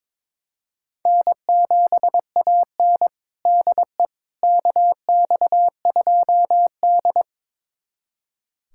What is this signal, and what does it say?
Morse code, heard as a keyed tone, about 22 wpm, 700 Hz. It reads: N7AN DE KX2D
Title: USA Callsign DE Callsign - 4 Characters - 22wpm